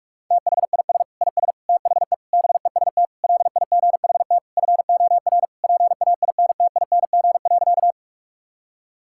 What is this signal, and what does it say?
Morse code, heard as a keyed tone, about 41 wpm, 700 Hz. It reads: THIS IS THE BEST LIGHT FOR PAINTING.
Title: Sentences from Top 1000 Words - Review Part 6 - 45wpm